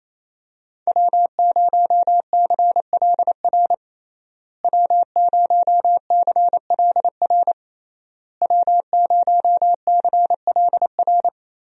Morse code, heard as a keyed tone, about 27 wpm, 700 Hz. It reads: W0CLR W0CLR W0CLR
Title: USA Call Signs - 5 Characters - Repeated 3x - 28wpm